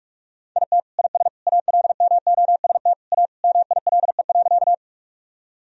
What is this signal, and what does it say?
Morse code, heard as a keyed tone, about 47 wpm, 700 Hz. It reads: IT IS ALMOST A MILE.
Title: Sentences from Top 1000 Words - Review Part 4 - 45wpm